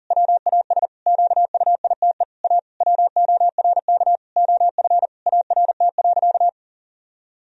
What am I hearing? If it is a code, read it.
Morse code, heard as a keyed tone, about 37 wpm, 700 Hz. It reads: WAS QUITE A WORK OF ART.
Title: Sentences from Top 1000 Words - Review Part 3 - 40wpm